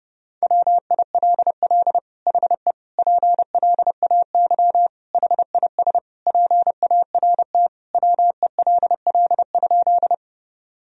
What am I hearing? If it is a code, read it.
Morse code, heard as a keyed tone, about 28 wpm, 700 Hz. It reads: WILL HE PLAY HIS PART WELL?
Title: Sentences from Top 1000 Words - Review Part 1 - 30wpm